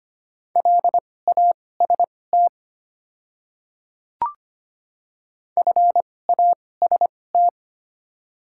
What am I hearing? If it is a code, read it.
Morse code, heard as a keyed tone, about 25 wpm, spaced as for 18 wpm, 700 Hz. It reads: LAST FAST
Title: 2x Character Spacing - Top 500 Words - 25wpm